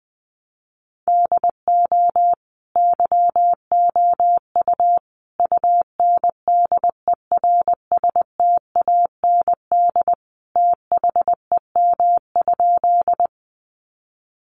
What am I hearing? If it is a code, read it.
Morse code, heard as a keyed tone, about 20 wpm, 700 Hz. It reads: DO YOU UNDERSTAND THEM?
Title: Sentences from Top 600 Words 20wpm